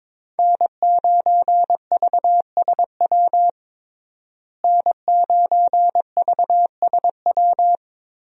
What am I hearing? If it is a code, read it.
Morse code, heard as a keyed tone, about 22 wpm, 700 Hz. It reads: N9VSW N9VSW
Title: USA Call Signs - 5 Characters - Repeated 2x - 22wpm